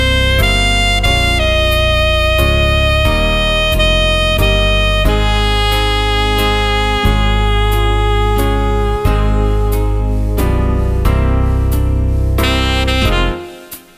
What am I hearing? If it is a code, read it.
Dial-up modem handshake.